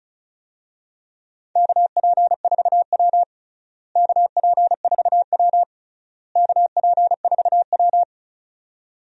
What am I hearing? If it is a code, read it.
Morse code, heard as a keyed tone, about 34 wpm, 700 Hz. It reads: KP4W KP4W KP4W
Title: DXCC-100 Call Signs - 4 Characters - Repeated 3x - 35wpm